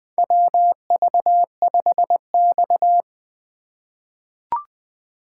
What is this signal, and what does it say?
Morse code, heard as a keyed tone, about 20 wpm, 700 Hz. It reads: WV5X
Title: USA Call Signs - 4 Characters - 4x Character Spacing - 20wpm